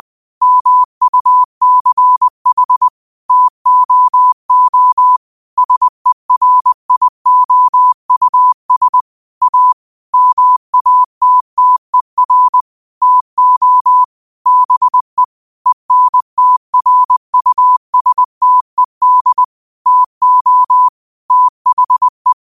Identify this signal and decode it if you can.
Morse code, heard as a keyed tone, about 20 wpm, 1000 Hz. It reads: MUCH TOO SERIOUS A MATTER TO BE ENTRUSTED TO THE
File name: qotdfiles / 20wpm